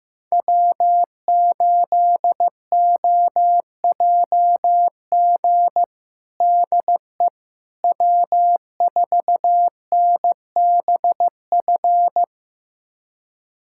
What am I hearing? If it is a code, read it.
Morse code, heard as a keyed tone, about 15 wpm, 700 Hz. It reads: W8OJG DE W4NBF
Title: USA Callsign DE Callsign - 5 Characters - 15wpm